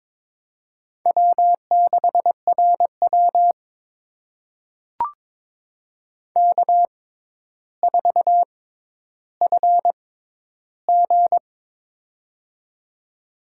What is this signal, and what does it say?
Morse code, heard as a keyed tone, about 22 wpm, spaced as for 11 wpm, 700 Hz. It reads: W6RW K4FG
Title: USA Call Signs - 4 Characters - 6x Character Spacing - 22wpm